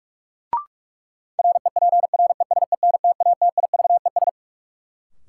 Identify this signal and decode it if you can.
Morse code, heard as a keyed tone, about 46 wpm, 700 Hz. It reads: REPRESENTATIVES
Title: Sets of 5 Words Encore - New Word Drill - 45wpm